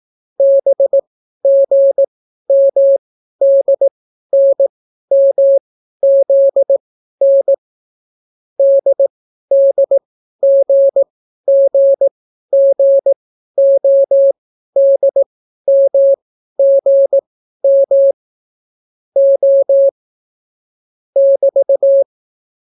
Morse code, heard as a keyed tone, about 18 wpm, spaced as for 12 wpm, 550 Hz. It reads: BGMDNMZN DDGGGODMGM O <BT>